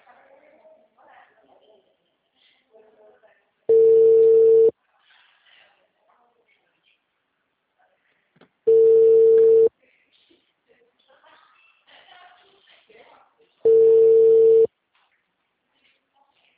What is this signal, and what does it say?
Ringback tone (China, 450 Hz), with speech.
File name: out- -2011- - - .17487